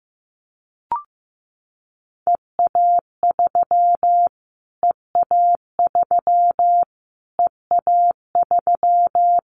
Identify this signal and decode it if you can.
Morse code, heard as a keyed tone, about 15 wpm, 700 Hz. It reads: EA3 EA3 EA3
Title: Callsign Prefixes for Top 25 DXCC - Repeated 3x - 15wpm